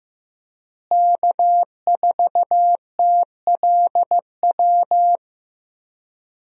Morse code, heard as a keyed tone, about 15 wpm, 700 Hz. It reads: K4TLW
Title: USA Call Signs - 5 Characters - Repeated 2x - 15wpm